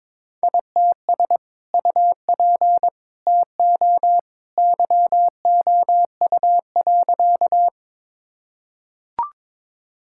Morse code, heard as a keyed tone, about 22 wpm, 700 Hz. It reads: ITS UP TO YOU.
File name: practice / 22wpm / Sentences with 2-Word Phrases - 22wpm